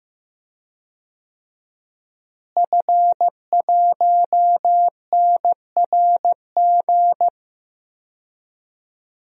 Morse code, heard as a keyed tone, about 14 wpm, 700 Hz. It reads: F1NRG
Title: DXCC-100 Call Signs - 5 Characters - Repeated 3x - 15wpm